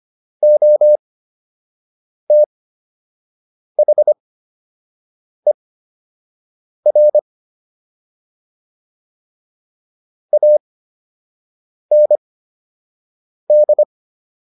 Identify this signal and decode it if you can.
Morse code, heard as a keyed tone, about 25 wpm, spaced as for 6 wpm, 600 Hz. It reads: OTHER AND